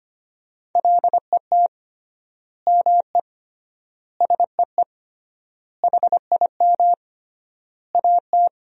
Morse code, heard as a keyed tone, about 25 wpm, 700 Hz. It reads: LET ME SEE HIM AT
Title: 3x Word-Spacing - Sentences from Top 300 Words - 25wpm